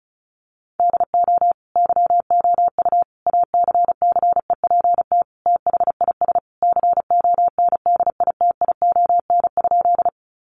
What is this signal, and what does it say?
Morse code, heard as a keyed tone, about 34 wpm, 700 Hz. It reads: DO YOU ACCEPT THIS CONDITION?
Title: Sentences from Top 1300 Words 35wpm